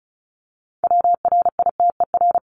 Morse code, heard as a keyed tone, about 34 wpm, 700 Hz. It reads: WRITER
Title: Top 1751-2000 Words 35wpm